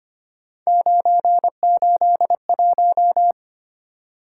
Morse code, heard as a keyed tone, about 25 wpm, 700 Hz. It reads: 981
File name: Three Letter-Number Combinations - 25wpm